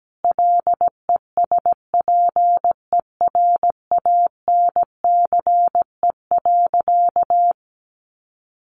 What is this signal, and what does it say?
Morse code, heard as a keyed tone, about 17 wpm, 700 Hz. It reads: LESPERANCE.
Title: A Sound Of Thunder 17wpm